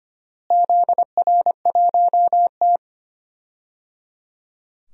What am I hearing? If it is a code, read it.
Morse code, heard as a keyed tone, about 25 wpm, 700 Hz. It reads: ZR1T
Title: DXCC-100 Call Signs - 4 Characters - Repeated 3x - 25wpm